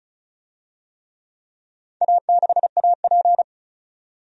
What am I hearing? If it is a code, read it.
Morse code, heard as a keyed tone, about 36 wpm, 690 Hz. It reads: A6AP
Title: DXCC-100 Call Signs - 4 Characters - 35wpm